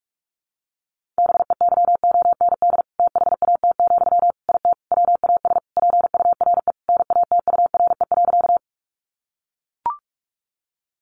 Morse code, heard as a keyed tone, about 44 wpm, 700 Hz. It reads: BEYOND THAT, IT WAS PURE NATURE.
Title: Sentences from Top 1500 Words 45wpm